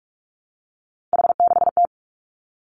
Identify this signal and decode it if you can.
Morse code, heard as a keyed tone, about 46 wpm, 700 Hz. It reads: H6T
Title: Three Challenging Letter-Number Combinations 45wpm